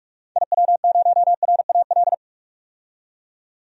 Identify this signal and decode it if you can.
Morse code, heard as a keyed tone, about 44 wpm, 700 Hz. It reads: IW0RAL
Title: DXCC-100 Call Signs - 6 Characters - 45wpm